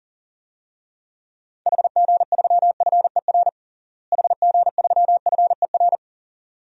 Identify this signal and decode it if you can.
Morse code, heard as a keyed tone, about 37 wpm, 700 Hz. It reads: HG3FER HG3FER
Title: DXCC-100 Call Signs - 6 Characters - Repeated 2x - 40wpm